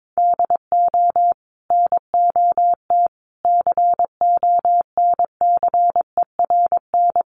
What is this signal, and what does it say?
Morse code, heard as a keyed tone, about 22 wpm, 700 Hz. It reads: DO NOT CONCERN
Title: Sentences from Top 1000 Words 22wpm